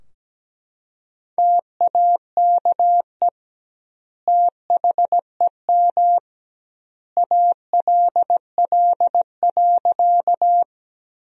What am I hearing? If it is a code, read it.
Morse code, heard as a keyed tone, about 17 wpm, 700 Hz. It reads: TAKE THEM ALL.
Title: 2x Word-Spacing - Sentences from Top 200 Words - 17wpm